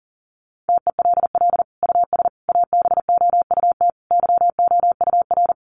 Morse code, heard as a keyed tone, about 40 wpm, 700 Hz. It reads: TELL US ABOUT YOUR
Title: Sentences from Top 1200 Words 40wpm